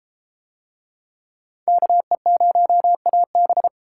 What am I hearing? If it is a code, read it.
Morse code, heard as a keyed tone, about 32 wpm, 700 Hz. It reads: KE0AB